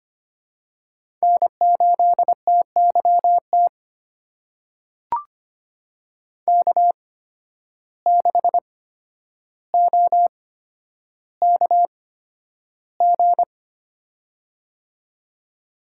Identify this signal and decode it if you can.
Morse code, heard as a keyed tone, about 25 wpm, 700 Hz. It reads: N 8 T Y T K 6 O K G
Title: USA Call Signs - 8x Character Spacing - 25wpm